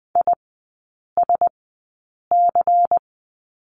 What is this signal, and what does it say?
Morse code, heard as a keyed tone, about 20 wpm, spaced as for 8 wpm, 700 Hz. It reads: ISC
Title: 114 Lesson 19e - Letter w - Three Characters 20wpm